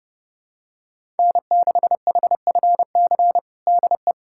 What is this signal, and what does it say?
Morse code, heard as a keyed tone, about 31 wpm, 700 Hz. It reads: N6HFC DE
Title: USA Callsign DE Callsign - 5 Characters - 30wpm